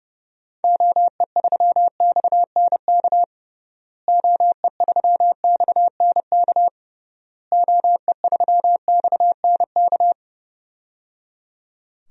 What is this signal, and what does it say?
Morse code, heard as a keyed tone, about 31 wpm, 700 Hz. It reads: OE3XNK OE3XNK OE3XNK
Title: DXCC-100 Call Signs - 6 Characters - Repeated 3x - 30wpm